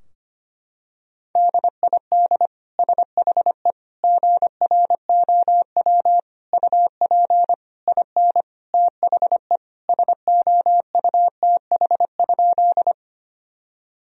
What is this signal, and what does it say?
Morse code, heard as a keyed tone, about 25 wpm, 700 Hz. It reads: DID SHE GROW UP IN THE SOUTH?
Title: Sentences from Top 1000 Words - Review Part 2 - 25wpm